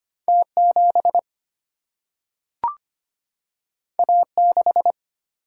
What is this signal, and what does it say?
Morse code, heard as a keyed tone, about 25 wpm, 700 Hz. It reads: T7 A6
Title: Single-Letter Single-Number 25wpm